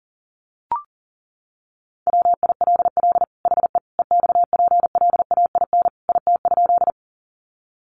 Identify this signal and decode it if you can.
Morse code, heard as a keyed tone, about 40 wpm, 700 Hz. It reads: WILL HE EXPLAIN IT?
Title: Sentences from Top 1100 Words 40wpm